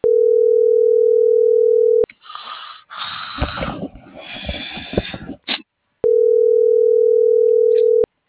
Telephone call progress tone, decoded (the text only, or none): ringback tone